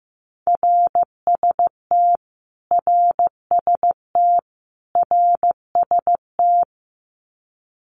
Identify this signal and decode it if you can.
Morse code, heard as a keyed tone, about 15 wpm, 700 Hz. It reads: RST RST RST
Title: Abbreviations Repeated 3x - 15wpm